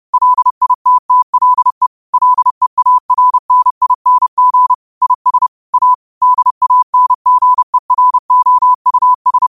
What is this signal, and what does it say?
Morse code, heard as a keyed tone, about 31 wpm, 1000 Hz. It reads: LITTLE LEARNING IS A DANGEROUS
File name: qotdfiles / 30wpm